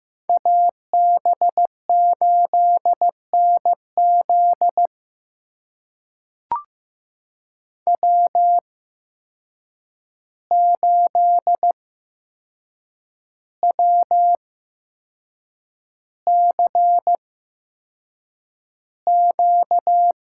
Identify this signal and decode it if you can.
Morse code, heard as a keyed tone, about 15 wpm, spaced as for 6 wpm, 700 Hz. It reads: AB8NZ W8WCQ
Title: USA Call Signs - 5 Characters - 8x Character Spacing - 15wpm